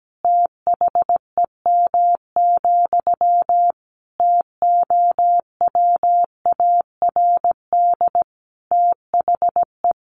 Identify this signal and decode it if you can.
Morse code, heard as a keyed tone, about 17 wpm, 700 Hz. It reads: THEM, TOWARD THE
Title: A Sound Of Thunder 17wpm